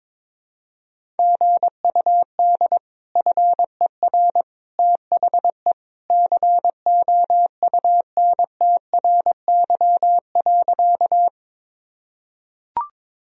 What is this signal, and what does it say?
Morse code, heard as a keyed tone, about 22 wpm, 700 Hz. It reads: GUD FER THE COUNTRY.
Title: Sentences from Top 300 Words and Abbreviations 22wpm